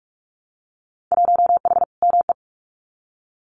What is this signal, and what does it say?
Morse code, heard as a keyed tone, about 44 wpm, 700 Hz. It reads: 1H ME
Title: Winter Field Day Exchanges 45wpm